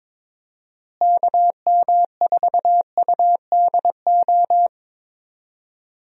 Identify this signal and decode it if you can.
Morse code, heard as a keyed tone, about 22 wpm, 700 Hz. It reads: KM4UDO
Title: USA Call Signs - 6 Characters - Repeated 2x - 22wpm